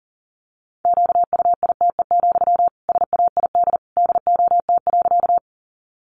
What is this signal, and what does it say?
Morse code, heard as a keyed tone, about 40 wpm, 700 Hz. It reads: QUITE, SAID DOT.
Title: Sentences from Top 1500 Words 40wpm